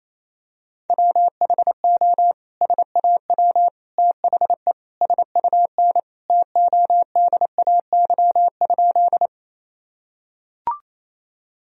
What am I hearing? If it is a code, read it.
Morse code, heard as a keyed tone, about 27 wpm, 700 Hz. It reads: WHO SAW THE SUN TODAY?
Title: Sentences from Top 1000 Words - Review Part 1 - 28wpm